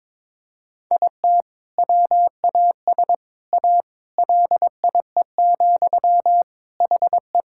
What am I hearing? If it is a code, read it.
Morse code, heard as a keyed tone, about 22 wpm, 700 Hz. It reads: IT WAS A LIE, HE
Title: Sentences from Top 1600 Words 22wpm